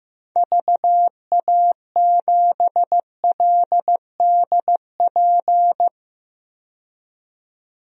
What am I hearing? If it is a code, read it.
Morse code, heard as a keyed tone, about 16 wpm, 700 Hz. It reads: VA7LDP
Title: DXCC-100 Call Signs - 6 Characters - 15wpm